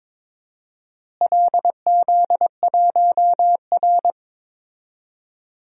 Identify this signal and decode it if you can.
Morse code, heard as a keyed tone, about 22 wpm, 700 Hz. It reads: LZ1R
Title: DXCC-100 Call Signs - 4 Characters - Repeated 2x - 22wpm